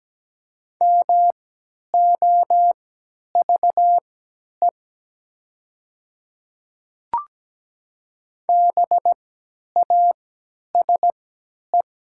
Morse code, heard as a keyed tone, about 17 wpm, spaced as for 10 wpm, 700 Hz. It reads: MOVE BASE
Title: 3x Character Spacing - Top 300 Words - 17wpm